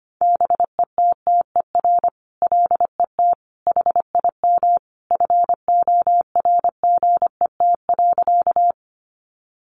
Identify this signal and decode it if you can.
Morse code, heard as a keyed tone, about 25 wpm, 700 Hz. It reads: BETTER LET HIM FORGET.
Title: Sentences from Top 1200 Words 25wpm